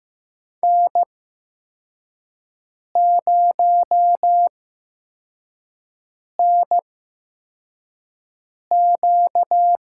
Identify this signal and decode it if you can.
Morse code, heard as a keyed tone, about 15 wpm, spaced as for 4 wpm, 700 Hz. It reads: N0NQ
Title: USA Call Signs - 5 Characters - 8x Character Spacing - 15wpm